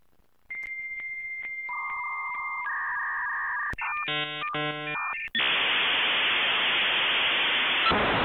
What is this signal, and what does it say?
Dial-up modem handshake.
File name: The-Sound-of-dial-up-Internet2